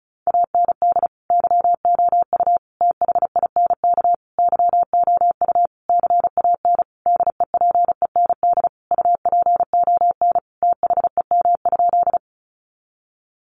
Morse code, heard as a keyed tone, about 34 wpm, 700 Hz. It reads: AND YOU THINK YOU CAN DEPEND UPON THEM?
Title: Sentences from Top 1300 Words 35wpm